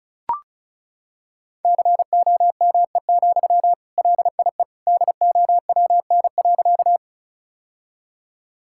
Morse code, heard as a keyed tone, about 34 wpm, 700 Hz. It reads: COME, LIE DOWN.
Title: Sentences from Top 1600 Words 35wpm